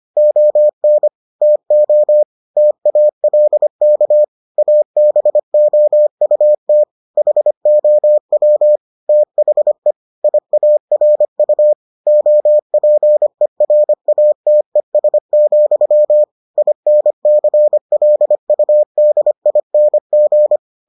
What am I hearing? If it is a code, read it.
Morse code, heard as a keyed tone, about 25 wpm, 600 Hz. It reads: ON TO TALK ABOUT HOW THE IARU OPERATES, INCLUDING